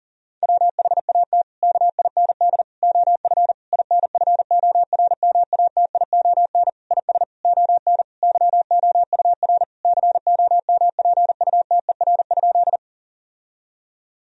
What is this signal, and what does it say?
Morse code, heard as a keyed tone, about 37 wpm, 700 Hz. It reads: WHAT KIND OF INFORMATION IS ON YOUR COMPUTER?
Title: Sentences from Top 1000 Words - Review Part 3 - 40wpm